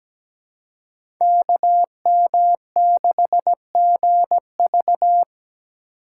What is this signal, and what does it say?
Morse code, heard as a keyed tone, about 17 wpm, 700 Hz. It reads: KM6GV
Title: USA Call Signs - 5 Characters - 2x Character Spacing - 17wpm